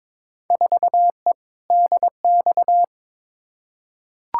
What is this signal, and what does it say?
Morse code, heard as a keyed tone, about 22 wpm, 700 Hz. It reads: 4E DX
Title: Field Day Exchanges Encore - 22wpm